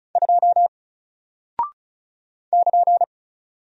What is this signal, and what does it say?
Morse code, heard as a keyed tone, about 34 wpm, 700 Hz. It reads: 2 <KN>
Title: Single Character (Letter:Number:Punctuation:Prosign) 35wpm